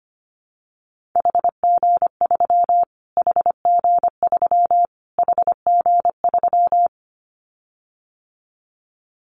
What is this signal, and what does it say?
Morse code, heard as a keyed tone, about 25 wpm, 700 Hz. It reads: HG3 HG3 HG3
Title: Callsign Prefixes for Top 25 DXCC - Repeated 3x - 25wpm